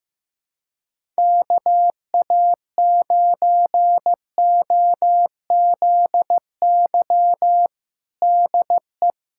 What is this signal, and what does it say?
Morse code, heard as a keyed tone, about 15 wpm, 700 Hz. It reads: KA9OZY DE
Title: USA Callsign DE Callsign - 6 Characters - 15wpm